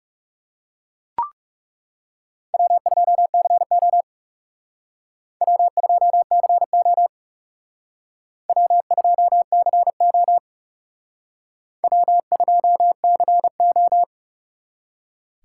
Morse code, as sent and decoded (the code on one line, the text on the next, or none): .-- ..--- -.-. --- / .-- ..--- -.-. --- / .-- ..--- -.-. --- / .-- ..--- -.-. ---
W2CO W2CO W2CO W2CO